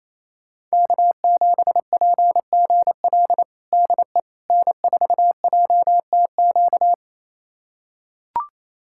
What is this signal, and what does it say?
Morse code, heard as a keyed tone, about 28 wpm, 700 Hz. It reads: K7PGL DE N4JTQ
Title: USA Callsign DE Callsign - 5 Characters - 28wpm